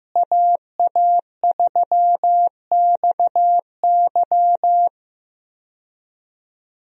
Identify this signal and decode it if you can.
Morse code, heard as a keyed tone, about 16 wpm, 700 Hz. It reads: AA3XY
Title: USA Call Signs - 5 Characters - Repeated 2x - 15wpm